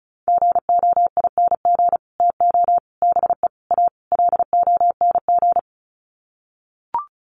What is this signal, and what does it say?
Morse code, heard as a keyed tone, about 34 wpm, 700 Hz. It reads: GOING TO BE A LONG
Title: Sets of 5 Words Encore2 35wpm